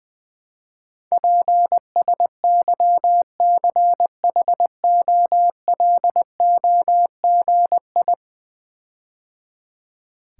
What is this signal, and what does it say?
Morse code, heard as a keyed tone, about 20 wpm, 700 Hz. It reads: PSYCHOLOGI
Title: Build A Word - 5 Letters At A Time - 12-Letter Words - 20wpm